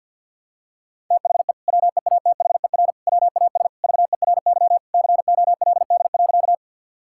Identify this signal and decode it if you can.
Morse code, heard as a keyed tone, about 51 wpm, 700 Hz. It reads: THE WEATHER WAS VERY COLD.
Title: Sentences from Top 1000 Words - Review Part 5 - 50wpm